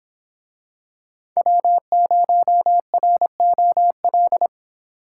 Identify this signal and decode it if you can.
Morse code, heard as a keyed tone, about 26 wpm, 700 Hz. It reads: W0ROL